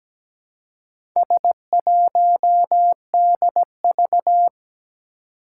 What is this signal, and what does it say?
Morse code, heard as a keyed tone, about 17 wpm, 700 Hz. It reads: S1DV
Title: Four Challenging Letter-Number Combinations 17wpm